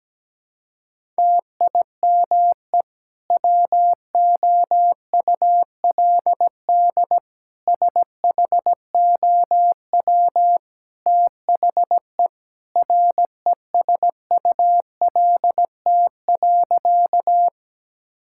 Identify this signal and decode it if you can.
Morse code, heard as a keyed tone, about 17 wpm, 700 Hz. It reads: TIME WOULD SHOW THE RESULT.
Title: Sentences from Top 1000 Words - Review Part 5 - 17wpm